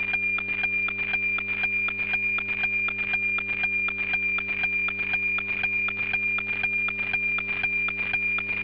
Weather fax (radiofax).